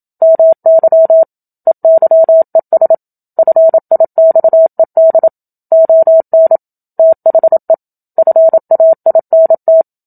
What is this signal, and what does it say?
Morse code, heard as a keyed tone, about 27 wpm, 650 Hz. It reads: MY EYES FIXED ON THE FAINT
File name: Book Two / 01-Chapter 1 - 27 WPM